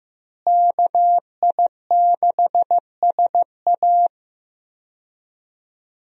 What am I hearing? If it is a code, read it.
Morse code, heard as a keyed tone, about 15 wpm, 700 Hz. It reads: KI6SA
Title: USA Call Signs - 5 Characters - Repeated 2x - 15wpm